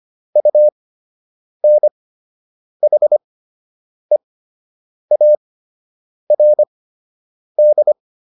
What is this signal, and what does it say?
Morse code, heard as a keyed tone, about 25 wpm, spaced as for 8 wpm, 600 Hz. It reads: UNHEARD